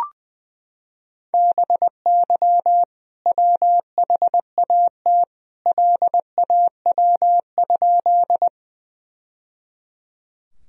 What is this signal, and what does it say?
Morse code, heard as a keyed tone, about 21 wpm, 700 Hz. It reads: BY WHAT LAW?
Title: Sentences from Top 1000 Words - Review Part 3 - 20wpm